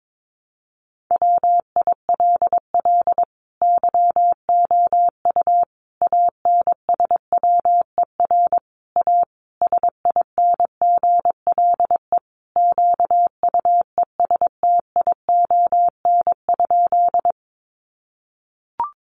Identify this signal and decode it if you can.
Morse code, heard as a keyed tone, about 22 wpm, 700 Hz. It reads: WILL YOU ANSWER A SINGLE QUESTION?
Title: Sentences from Top 700 Words 22wpm